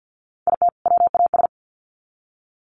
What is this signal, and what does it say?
Morse code, heard as a keyed tone, about 50 wpm, 700 Hz. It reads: IT WAS
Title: Sets of 2 Words 50wpm